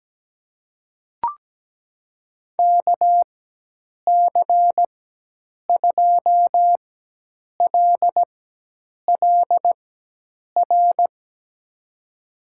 Morse code, heard as a keyed tone, about 17 wpm, spaced as for 8 wpm, 700 Hz. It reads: KC2LLR
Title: USA Call Signs - 6 Characters - 4x Character Spacing - 17wpm